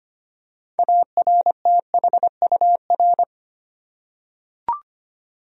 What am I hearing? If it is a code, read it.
Morse code, heard as a keyed tone, about 25 wpm, 700 Hz. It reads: ARTHUR
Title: USA Common Names - 25wpm